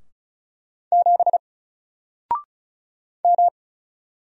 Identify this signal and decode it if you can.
Morse code, heard as a keyed tone, about 35 wpm, 700 Hz. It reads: 7 M
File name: Single-Letter-Number-35wpm